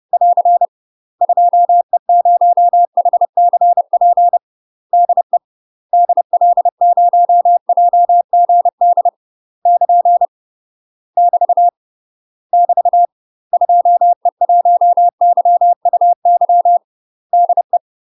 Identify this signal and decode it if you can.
Morse code, heard as a keyed tone, about 30 wpm, 700 Hz. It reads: <AR> 2E0HCP DE DL0JGD <KN> <BT> <BT> 2E1YUY DE
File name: UK MASTER QSOs TRAINING / 30WPM / UK MASTER QSOs 0036